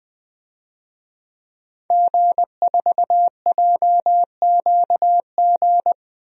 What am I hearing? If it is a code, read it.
Morse code, heard as a keyed tone, about 20 wpm, 700 Hz. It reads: G4JQG
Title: DXCC-100 Call Signs - 5 Characters - 20wpm